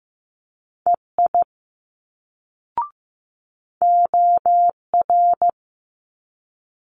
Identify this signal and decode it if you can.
Morse code, heard as a keyed tone, about 15 wpm, 700 Hz. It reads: EI OR
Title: Multi-letter Phonemes Quick Fire 15wpm